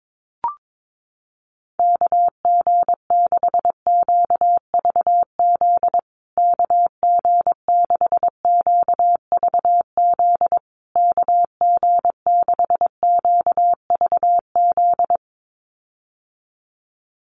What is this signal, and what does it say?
Morse code, heard as a keyed tone, about 22 wpm, 700 Hz. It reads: KG6QVZ KG6QVZ KG6QVZ
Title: USA Call Signs Repeated 3x 22wpm